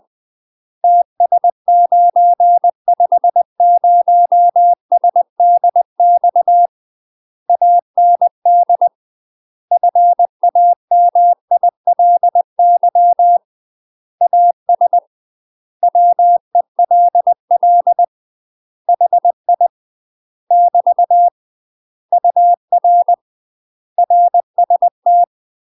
Morse code, heard as a keyed tone, about 20 wpm, 700 Hz. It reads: TS950SDX AND FAMILY AS WELL HI <BT> UR RST